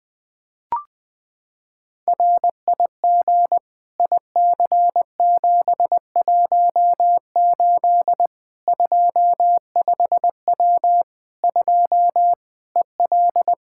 Morse code, heard as a keyed tone, about 21 wpm, 700 Hz. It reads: RIG IC718 25W 2 EL
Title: Practice QSO - Exchange 2 - Info - 20wpm